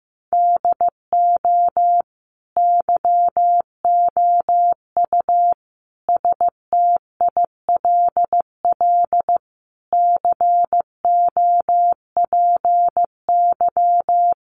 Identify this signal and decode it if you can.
Morse code, heard as a keyed tone, about 15 wpm, 700 Hz. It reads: DO YOU STILL COPY
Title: Sentences from Top 1400 Words 15wpm